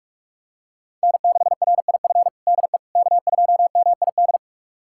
Morse code, heard as a keyed tone, about 45 wpm, 700 Hz. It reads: N6RIF DE K2MID
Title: USA Callsign DE Callsign - 5 Characters - 45wpm